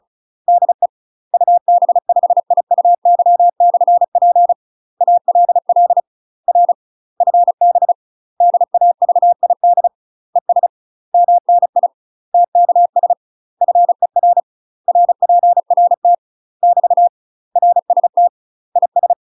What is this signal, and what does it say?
Morse code, heard as a keyed tone, about 34 wpm, 700 Hz. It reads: DE UB5IUY/P ALL R FB DAVID ES MNI TKS FER RPRT <BT> RST IS
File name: UK MASTER QSOs TRAINING / 35WPM / UK MASTER QSOs 0042